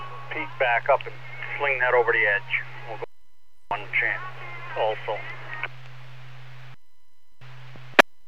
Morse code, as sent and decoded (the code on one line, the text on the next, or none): none